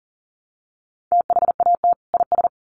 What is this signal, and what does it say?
Morse code, heard as a keyed tone, about 40 wpm, 700 Hz. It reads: THAT IS
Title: Sentences from Top 800 Words 40wpm